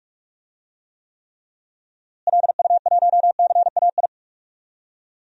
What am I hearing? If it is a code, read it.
Morse code, heard as a keyed tone, about 45 wpm, 700 Hz. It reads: LU1CAI
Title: DXCC-100 Call Signs - 6 Characters - Repeated 3x - 45wpm